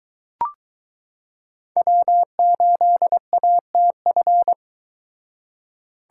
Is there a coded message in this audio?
Yes: Morse code from 1.8 to 4.5 s, heard as a keyed tone, about 22 wpm, 700 Hz. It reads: W8ATF